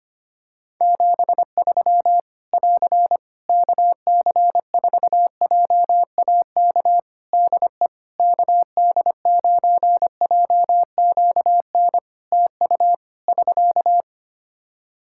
Morse code, heard as a keyed tone, about 25 wpm, 700 Hz. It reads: 73 <AR> KC4JAK DE KD9JQN TU <SK>
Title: Practice QSO - Exchange 4 - Ending - 25wpm